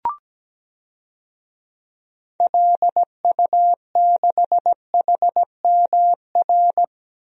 Morse code, heard as a keyed tone, about 17 wpm, 700 Hz. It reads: LU6HMR
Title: DXCC-100 Call Signs - 6 Characters - Repeated 3x - 17wpm